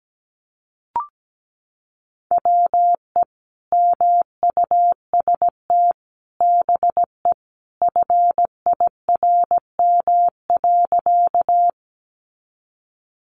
Morse code, heard as a keyed tone, about 17 wpm, 700 Hz. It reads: WE MUST BE FIRM.
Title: Sentences from Top 1400 Words 17wpm